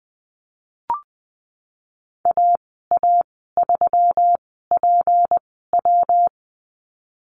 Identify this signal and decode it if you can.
Morse code, heard as a keyed tone, about 20 wpm, spaced as for 15 wpm, 700 Hz. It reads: AA3PW
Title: 154 Lesson 25g - Letter p - Call Signs 20wpm